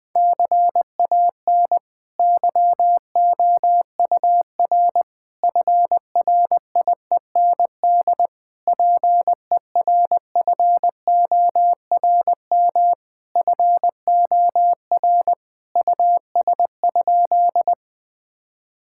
Morse code, heard as a keyed tone, about 20 wpm, 700 Hz. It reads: CAN YOUR FRIEND PERFORM FOR US?
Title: Sentences from Top 1000 Words - Review Part 5 - 20wpm